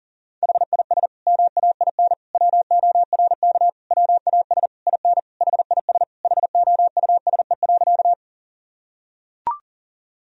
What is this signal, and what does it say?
Morse code, heard as a keyed tone, about 40 wpm, 700 Hz. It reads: HIS MAIN WORK WAS IN HIS HOUSE.
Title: Sentences from Top 1000 Words - Review Part 2 - 40wpm